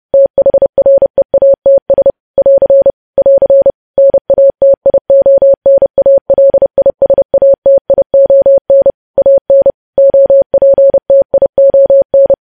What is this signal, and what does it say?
Morse code, heard as a keyed tone, about 30 wpm, 550 Hz. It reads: THREATS <AR> <AR> NATIONALISATION AN OPTION
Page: news headlines in morse code at 30 wpm